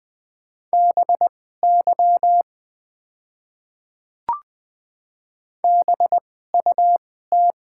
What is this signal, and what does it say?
Morse code, heard as a keyed tone, about 20 wpm, spaced as for 14 wpm, 700 Hz. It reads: BY BUT
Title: 2x Character Spacing - Top 100 Words - 20wpm